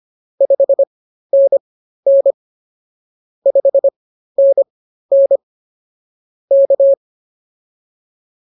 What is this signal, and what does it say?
Morse code, heard as a keyed tone, about 24 wpm, spaced as for 13 wpm, 550 Hz. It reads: 5NN 5NN K